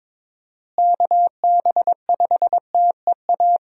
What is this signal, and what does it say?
Morse code, heard as a keyed tone, about 21 wpm, 700 Hz. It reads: KB5TEA
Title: USA Call Signs - 6 Characters - Repeated 3x - 22wpm